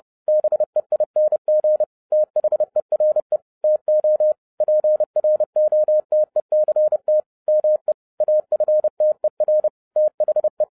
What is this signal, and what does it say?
Morse code, heard as a keyed tone, about 30 wpm, 610 Hz. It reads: BEING THERE TO PROTECT ME AFTER THE
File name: 2011-07-mp3 / n60 BACK FROM A BIG ACCIDENT 30wpm